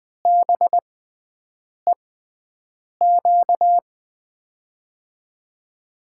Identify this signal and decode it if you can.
Morse code, heard as a keyed tone, about 20 wpm, spaced as for 7 wpm, 700 Hz. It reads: BEQ
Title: USA Call Signs - 6x Character Spacing - 20wpm